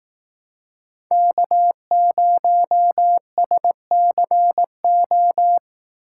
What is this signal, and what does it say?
Morse code, heard as a keyed tone, about 18 wpm, 700 Hz. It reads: K0SCO
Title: Speed Racing - USA 1x3 Call Signs 15wpm